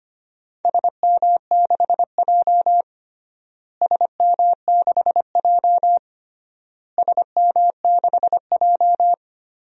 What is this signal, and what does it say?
Morse code, heard as a keyed tone, about 25 wpm, 700 Hz. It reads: SM6J SM6J SM6J
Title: DXCC-100 Call Signs - 4 Characters - Repeated 3x - 25wpm